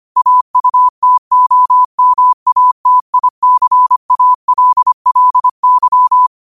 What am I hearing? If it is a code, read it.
Morse code, heard as a keyed tone, about 25 wpm, 1000 Hz. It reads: AUTOMATICALLY
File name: qotdfiles / 25wpm